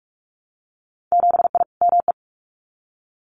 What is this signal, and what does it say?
Morse code, heard as a keyed tone, about 46 wpm, 700 Hz. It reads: 7I ME
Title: Winter Field Day Exchanges Encore 45wpm